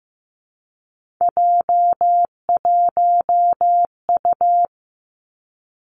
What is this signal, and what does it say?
Morse code, heard as a keyed tone, about 15 wpm, 700 Hz. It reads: J1U
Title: Three Challenging Letter-Number Combinations 15wpm